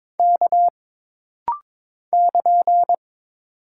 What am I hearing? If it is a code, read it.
Morse code, heard as a keyed tone, about 22 wpm, 700 Hz. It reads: K <KN>
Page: This rapid-fire timing is geared to push your ICR (Instant Character Recognition) skills.